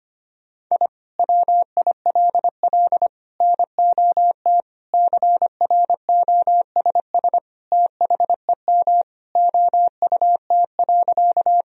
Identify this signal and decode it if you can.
Morse code, heard as a keyed tone, about 25 wpm, 700 Hz. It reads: I WILL NOT CROSS THEM OUT.
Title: Sentences from Top 1000 Words - Review Part 1 - 25wpm